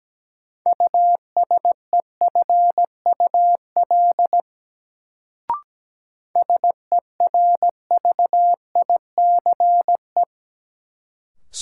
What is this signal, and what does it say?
Morse code, heard as a keyed tone, about 17 wpm, 700 Hz. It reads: USEFUL SERVICE
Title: Sentences from Top 700 Words - New Word Drill - 17wpm